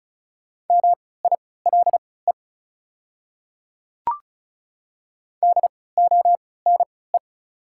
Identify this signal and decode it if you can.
Morse code, heard as a keyed tone, about 34 wpm, spaced as for 20 wpm, 690 Hz. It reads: MILE DONE
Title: 3x Character Spacing - Top 500 Words - 35wpm